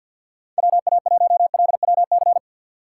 Morse code, heard as a keyed tone, about 52 wpm, 700 Hz. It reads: WA1LWC
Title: K1USN SST QSO Practice - 50wpm